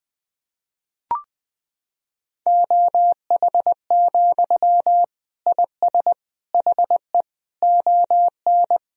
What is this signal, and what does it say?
Morse code, heard as a keyed tone, about 20 wpm, 700 Hz. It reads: OH, IS HE ON
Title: Sentences from Top 1000 Words - Review Part 1 - 20wpm